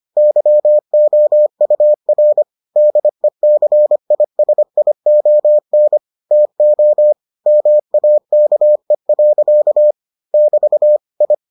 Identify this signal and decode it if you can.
Morse code, heard as a keyed tone, about 25 wpm, 600 Hz. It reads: YOUR DECISION TO MAKE. <BT> I